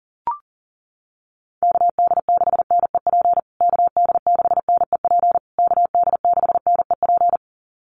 Morse code, heard as a keyed tone, about 40 wpm, 700 Hz. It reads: KD6NEP KD6NEP KD6NEP
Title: USA Call Signs Repeated 3x 40wpm